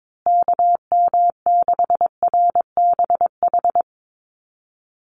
Morse code, heard as a keyed tone, about 22 wpm, 700 Hz. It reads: KM6RBH
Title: USA Call Signs Repeated 3x 22wpm